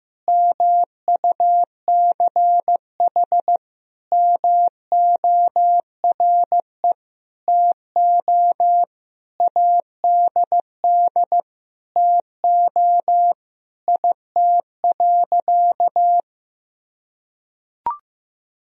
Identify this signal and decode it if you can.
Morse code, heard as a keyed tone, about 15 wpm, 700 Hz. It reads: MUCH MORE TO ADD TO IT.
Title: Sentences from Top 1000 Words - Review Part 1 - 15wpm